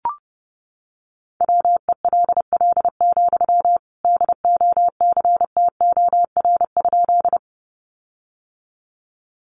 Morse code, heard as a keyed tone, about 30 wpm, 700 Hz. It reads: WELL, DOCTOR?
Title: Sentences from Top 1500 Words 30wpm